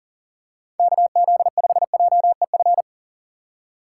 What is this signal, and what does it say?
Morse code, heard as a keyed tone, about 37 wpm, 700 Hz. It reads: KZ5JEF
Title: USA Call Signs - 6x Character Spacing - 40wpm